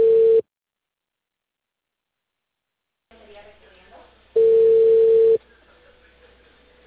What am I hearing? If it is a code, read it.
Ringback tone (China, 450 Hz), with speech.